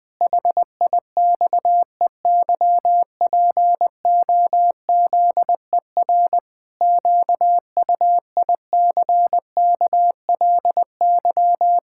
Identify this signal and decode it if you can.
Morse code, heard as a keyed tone, about 20 wpm, 700 Hz. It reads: HIXEYPOZER QUICKLY
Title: Pangram Challenge v2 - 20wpm